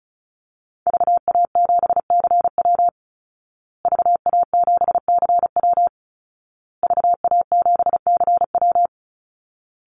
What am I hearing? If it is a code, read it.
Morse code, heard as a keyed tone, about 34 wpm, 700 Hz. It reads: VA7CW VA7CW VA7CW
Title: Call Signs Repeated 2x - 35wpm